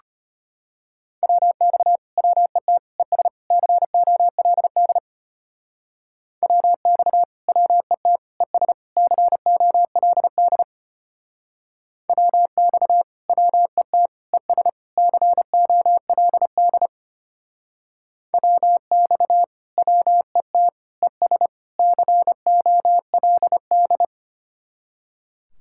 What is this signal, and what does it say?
Morse code, heard as a keyed tone, about 33 wpm, 700 Hz. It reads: WX WET ES COLD WX WET ES COLD WX WET ES COLD WX WET ES COLD